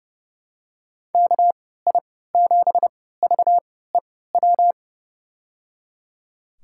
Morse code, heard as a keyed tone, about 30 wpm, spaced as for 17 wpm, 700 Hz. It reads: KI7VEW
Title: USA Call Signs - 6 Characters - 3x Character Spacing - 30wpm